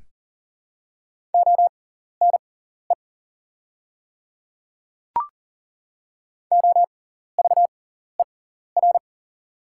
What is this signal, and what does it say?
Morse code, heard as a keyed tone, about 40 wpm, spaced as for 14 wpm, 700 Hz. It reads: ONE OVER
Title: 6x Character Spacing - Top 300 Words - 40wpm